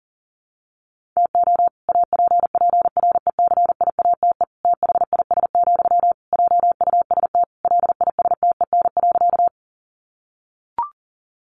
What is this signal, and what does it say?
Morse code, heard as a keyed tone, about 40 wpm, 700 Hz. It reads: TO APPRECIATE THIS, JUST LISTEN.
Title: Sentences from Top 1300 Words 40wpm